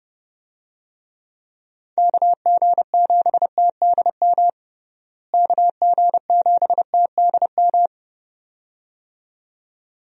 Morse code, heard as a keyed tone, about 32 wpm, 700 Hz. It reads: KG7TDM KG7TDM
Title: USA Call Signs - 6 Characters - Repeated 2x - 30wpm